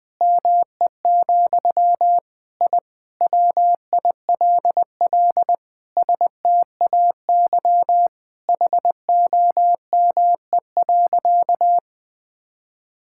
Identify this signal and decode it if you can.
Morse code, heard as a keyed tone, about 20 wpm, 700 Hz. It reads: ME, I WILL STAY HOME.
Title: Sentences from Top 1000 Words - Review Part 6 - 20wpm